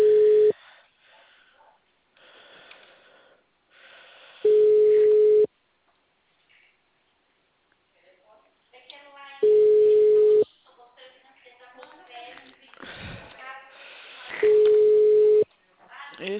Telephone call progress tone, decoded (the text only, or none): ringback tone